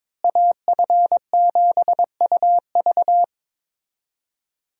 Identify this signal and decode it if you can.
Morse code, heard as a keyed tone, about 22 wpm, 700 Hz. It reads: AF7UV
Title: USA Call Signs - 5 Characters - Repeated 2x - 22wpm